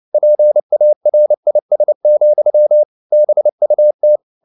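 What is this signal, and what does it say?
Morse code, heard as a keyed tone, about 29 wpm, 600 Hz. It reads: PARIS, BUT
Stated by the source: Morse code at 29 wpm, sending Ken Follett's novel Never in English, from CW med Gnister.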